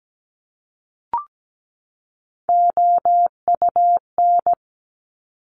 Morse code, heard as a keyed tone, about 17 wpm, 700 Hz. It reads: OUN
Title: Top Three Letter Combinations in Words 17wpm